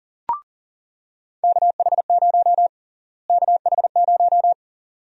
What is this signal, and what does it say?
Morse code, heard as a keyed tone, about 37 wpm, 700 Hz. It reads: KH0 KH0
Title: USA Call Sign Prefixes - Repeated 2x - 40wpm